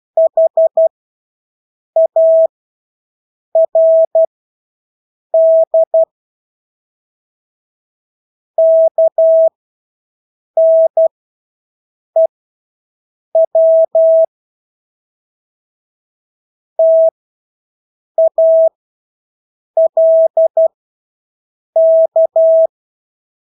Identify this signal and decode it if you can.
Morse code, heard as a keyed tone, about 12 wpm, spaced as for 6 wpm, 650 Hz. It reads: HARD KNEW TALK
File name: Engelske ord 12-6 wpm